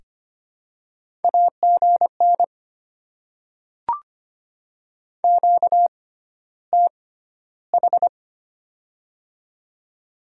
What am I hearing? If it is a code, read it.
Morse code, heard as a keyed tone, about 25 wpm, spaced as for 13 wpm, 700 Hz. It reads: AGN QTH
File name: Top 20 QSO Elements - 6x Character Spacing - 25wpm